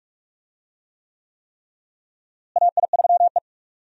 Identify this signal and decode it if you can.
Morse code, heard as a keyed tone, about 43 wpm, 700 Hz. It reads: AI3E
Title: USA Call Signs - 4 Characters - Repeated 3x - 45wpm